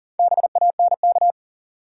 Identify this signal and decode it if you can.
Morse code, heard as a keyed tone, about 40 wpm, 700 Hz. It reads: BANK
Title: Sentences from Top 900 Words - New Word Drill - 40wpm